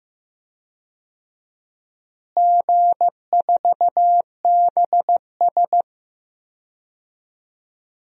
Morse code, heard as a keyed tone, about 15 wpm, 700 Hz. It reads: G4BS
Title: DXCC-100 Call Signs - 4 Characters - Repeated 2x - 15wpm